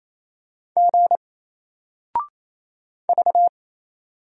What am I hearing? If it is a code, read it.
Morse code, heard as a keyed tone, about 27 wpm, 700 Hz. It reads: G V
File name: Single-Letter-Number-28wpm